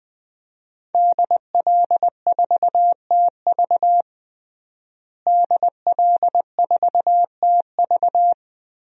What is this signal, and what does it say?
Morse code, heard as a keyed tone, about 20 wpm, 700 Hz. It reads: DL4TV DL4TV
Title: DXCC-100 Call Signs - 5 Characters - Repeated 3x - 20wpm